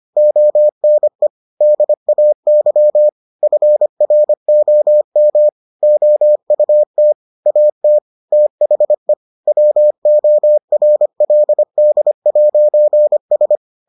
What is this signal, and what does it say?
Morse code, heard as a keyed tone, about 25 wpm, 600 Hz. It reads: ONE DAY FROM OUT AT THE WORLD'S